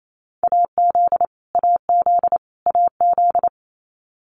Morse code, heard as a keyed tone, about 28 wpm, 700 Hz. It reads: AZ AZ AZ
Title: US States Repeated 3x 28wpm